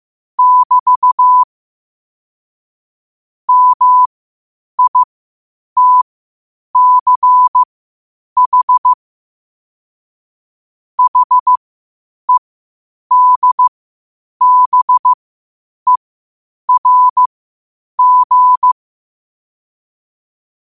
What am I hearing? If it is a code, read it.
Morse code, heard as a keyed tone, about 15 wpm, spaced as for 8 wpm, 1000 Hz. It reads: <BT> MITCH HEDBERG
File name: qotdfiles / 8wpm